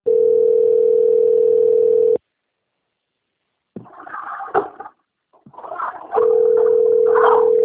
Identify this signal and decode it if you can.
Ringback tone (North America), 440 + 480 Hz.
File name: out- - - - - .37867